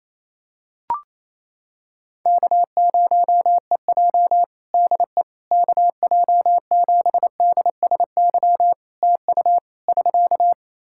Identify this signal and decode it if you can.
Morse code, heard as a keyed tone, about 27 wpm, 700 Hz. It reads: K0EJ DE KJ7DSY TU <SK>
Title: Practice QSO - Exchange 4 - Ending - 28wpm